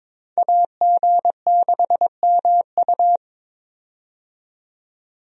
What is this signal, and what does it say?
Morse code, heard as a keyed tone, about 22 wpm, 700 Hz. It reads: AG6MU
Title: USA Call Signs - 5 Characters - 22wpm